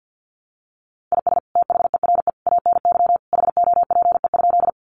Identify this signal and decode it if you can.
Morse code, heard as a keyed tone, about 50 wpm, 700 Hz. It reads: IS THERE ANY HOPE?
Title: Sentences from Top 1000 Words 50wpm